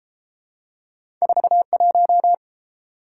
Morse code, heard as a keyed tone, about 29 wpm, 700 Hz. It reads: 41